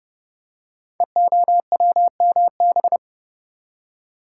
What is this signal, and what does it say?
Morse code, heard as a keyed tone, about 31 wpm, 700 Hz. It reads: EOWMB
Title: Code Groups - 5 Letter:Numbers - 30wpm